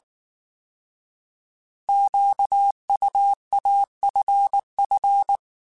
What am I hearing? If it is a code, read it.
Morse code, heard as a keyed tone, about 19 wpm, 800 Hz. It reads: QUAFF
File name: Alphabetical / Alphabetical mp3s / Qs @18